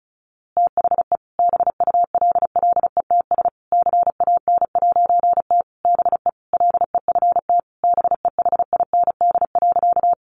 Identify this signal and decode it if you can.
Morse code, heard as a keyed tone, about 34 wpm, 700 Hz. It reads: THE BULLETS CAN'T BE LEFT BEHIND.
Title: A Sound Of Thunder 35wpm